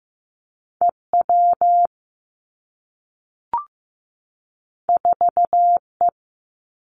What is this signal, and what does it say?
Morse code, heard as a keyed tone, about 15 wpm, 700 Hz. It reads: E W 4 E
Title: Two Challenging Letter-Number Combinations 15wpm